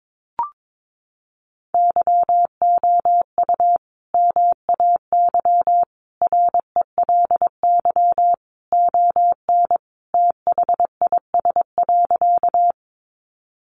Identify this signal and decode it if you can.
Morse code, heard as a keyed tone, about 22 wpm, 700 Hz. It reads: YOU MAY RELY ON THIS.
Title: Sentences from Top 1200 Words 22wpm